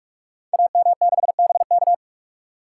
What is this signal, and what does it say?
Morse code, heard as a keyed tone, about 44 wpm, 700 Hz. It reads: AM6BK
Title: Code Groups - 5 Letter:Numbers - 45wpm